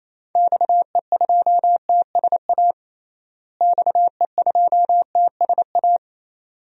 Morse code, heard as a keyed tone, about 28 wpm, 700 Hz. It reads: XE2TSA XE2TSA
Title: DXCC-100 Call Signs - 6 Characters - Repeated 2x - 28wpm